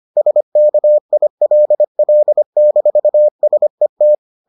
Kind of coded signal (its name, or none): Morse code